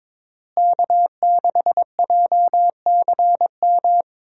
Morse code, heard as a keyed tone, about 22 wpm, 700 Hz. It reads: K6JCM
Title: USA Call Signs - 5 Characters - Repeated 2x - 22wpm